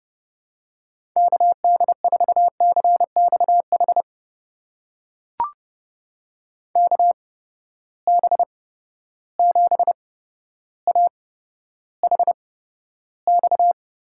Morse code, heard as a keyed tone, about 30 wpm, spaced as for 13 wpm, 700 Hz. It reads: KD4CXH KB7AHX
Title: USA Call Signs - 6 Characters - 8x Character Spacing - 30wpm